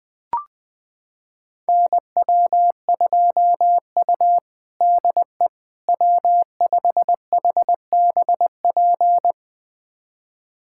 Morse code, heard as a keyed tone, about 21 wpm, 700 Hz. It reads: NW2U DE W5HBP
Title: USA Callsign DE Callsign 20wpm